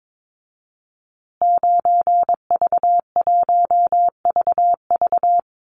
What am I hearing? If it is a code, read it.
Morse code, heard as a keyed tone, about 22 wpm, 700 Hz. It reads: 9V1VV
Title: Call Signs Repeated 2x - 22wpm